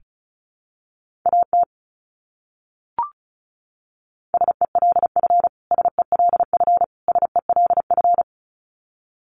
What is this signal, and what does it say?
Morse code, heard as a keyed tone, about 34 wpm, 700 Hz. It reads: AT SELF SELF SELF
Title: Top 200 Words Repeated 35wpm